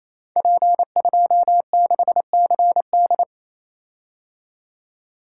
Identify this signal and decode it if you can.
Morse code, heard as a keyed tone, about 27 wpm, 700 Hz. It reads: P26CD
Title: Code Groups - 5 Letter:Numbers - 28wpm